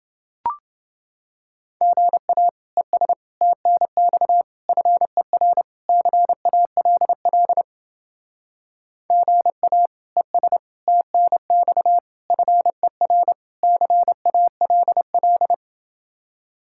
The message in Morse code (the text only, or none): GA ES TNX FER CALL GA ES TNX FER CALL